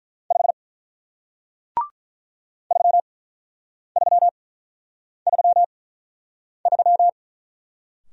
Morse code, heard as a keyed tone, about 47 wpm, spaced as for 9 wpm, 700 Hz. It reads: 5 3333